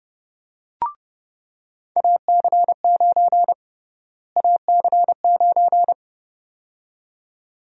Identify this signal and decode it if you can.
Morse code, heard as a keyed tone, about 32 wpm, 700 Hz. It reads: AC9 AC9
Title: USA Call Sign Prefixes - Repeated 2x - 30wpm